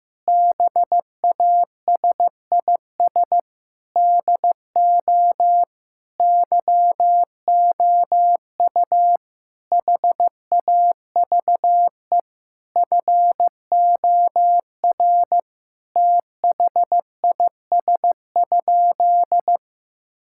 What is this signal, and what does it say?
Morse code, heard as a keyed tone, about 15 wpm, 700 Hz. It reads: BASIS DO YOU HAVE FOR THIS?
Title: Sentences from Top 1000 Words - Review Part 5 - 15wpm